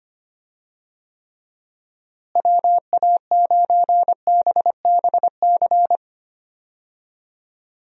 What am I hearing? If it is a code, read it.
Morse code, heard as a keyed tone, about 25 wpm, 700 Hz. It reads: WA9BBC
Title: USA Call Signs - 6 Characters - Repeated 2x - 25wpm